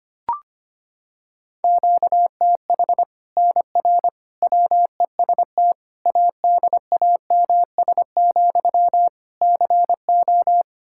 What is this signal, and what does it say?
Morse code, heard as a keyed tone, about 25 wpm, 700 Hz. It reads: QTH NR WEST ADAMS, CO
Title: Practice QSO - Exchange 1 - Beginning - 25wpm